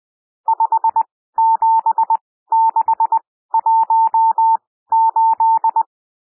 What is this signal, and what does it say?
Morse code, heard as a keyed tone, about 20 wpm, spaced as for 14 wpm, 900 Hz. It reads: 57618
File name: cwmp3 / N1 SOUND NUMBERS 20 wpm